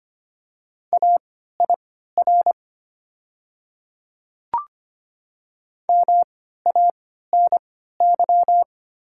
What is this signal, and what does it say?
Morse code, heard as a keyed tone, about 25 wpm, spaced as for 14 wpm, 690 Hz. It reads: AIR MANY
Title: 3x Character Spacing - Top 200 Words - 25wpm